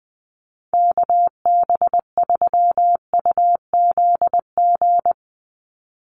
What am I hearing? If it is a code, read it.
Morse code, heard as a keyed tone, about 20 wpm, 700 Hz. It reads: KB3UZG
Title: USA Call Signs 20wpm